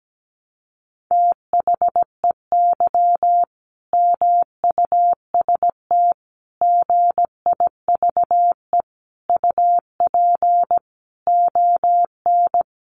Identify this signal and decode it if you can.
Morse code, heard as a keyed tone, about 17 wpm, 700 Hz. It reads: THEY MUST GIVE UP ON
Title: Sentences from Top 1100 Words 17wpm